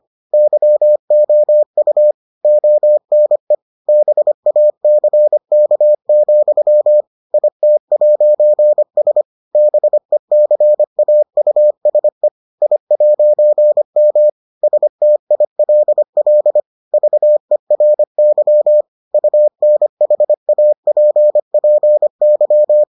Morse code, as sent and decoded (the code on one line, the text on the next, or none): -.-- --- ..- / --- -. / -... .- -.-. -.- --..-- / .. - .----. ... / -... -.-. .- ..- ... / .. .----. -- / ... - .. .-.. .-.. / ...- . .-. -.-- / ..- -. .... .- .--. .--. -.--
YOU ONE BACK, IT'S BECAUSE I'M STILL VERY UNHAPPY